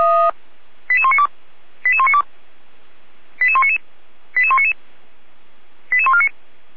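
Five-tone selective call (ZVEI), heard as quick runs of five tones, tones of about 70 ms. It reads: 88192 88192 88199 88199 88149
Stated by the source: ZVEI five-tone call 88327 received on SDR.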